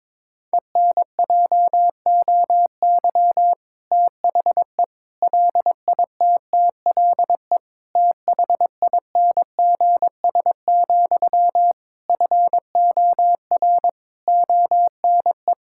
Morse code, heard as a keyed tone, about 22 wpm, 700 Hz. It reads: ENJOY THE LITTLE THINGS, FOR ONE
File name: Difficult Sentences with Hints - 22wpm